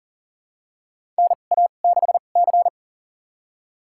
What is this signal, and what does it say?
Morse code, heard as a keyed tone, about 40 wpm, spaced as for 29 wpm, 700 Hz. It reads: NA6C
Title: USA Call Signs - 4 Characters - 2x Character Spacing - 40wpm